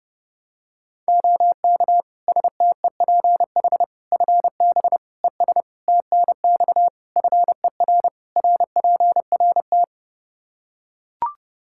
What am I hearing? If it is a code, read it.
Morse code, heard as a keyed tone, about 30 wpm, 700 Hz. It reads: OK STEPH FB ES TNX FER RPRT
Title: Practice QSO - Exchange 2 - Info - 30wpm